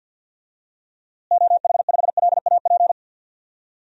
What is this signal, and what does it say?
Morse code, heard as a keyed tone, about 50 wpm, 700 Hz. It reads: OH5LAP
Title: DXCC-100 Call Signs - 6 Characters - Repeated 3x - 50wpm